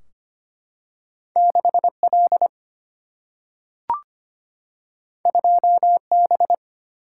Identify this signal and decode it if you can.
Morse code, heard as a keyed tone, about 25 wpm, 700 Hz. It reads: 6 L 2 B
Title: Single-Letter Single-Number 25wpm